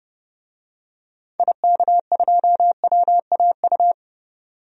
Morse code, heard as a keyed tone, about 30 wpm, 700 Hz. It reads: IK2WAU
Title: DXCC-100 Call Signs - 6 Characters - 30wpm